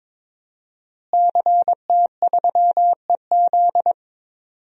Morse code, heard as a keyed tone, about 22 wpm, 700 Hz. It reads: CT3EZ
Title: DXCC-100 Call Signs - 5 Characters - Repeated 3x - 22wpm